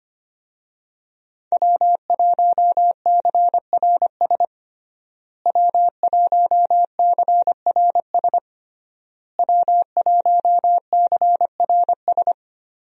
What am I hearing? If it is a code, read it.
Morse code, heard as a keyed tone, about 25 wpm, 700 Hz. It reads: W1CRS W1CRS W1CRS
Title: USA Call Signs - 5 Characters - Repeated 3x - 25wpm